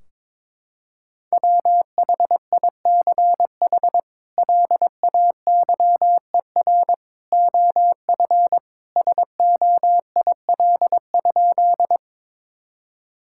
Morse code, heard as a keyed tone, about 22 wpm, 700 Hz. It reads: WHICH LAYER OF SOIL?
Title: Sentences from Top 1600 Words 22wpm